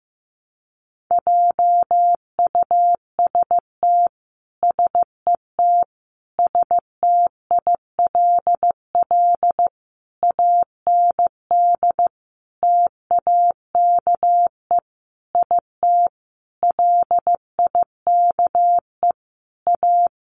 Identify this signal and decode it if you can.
Morse code, heard as a keyed tone, about 15 wpm, 700 Hz. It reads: JUST SET STILL AND TAKE IT LIKE A
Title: Sentences from Top 300 Words 15wpm